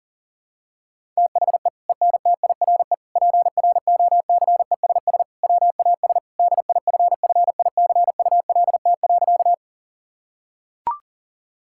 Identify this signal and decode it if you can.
Morse code, heard as a keyed tone, about 40 wpm, 700 Hz. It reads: THE ENTIRE PROCESS WAS DIFFICULT.
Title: Sentences from Top 1000 Words - Review Part 5 - 40wpm